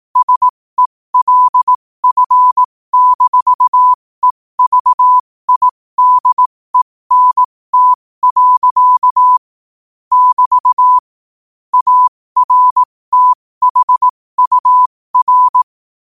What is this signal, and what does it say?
Morse code, heard as a keyed tone, about 18 wpm, spaced as for 15 wpm, 1000 Hz. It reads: SELF-EVIDENT. <BT> ARTHUR
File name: qotdfiles / 15wpm